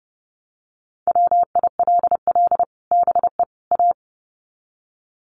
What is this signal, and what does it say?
Morse code, heard as a keyed tone, about 28 wpm, 700 Hz. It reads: WILL BE A
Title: Sets of 3 Words 30wpm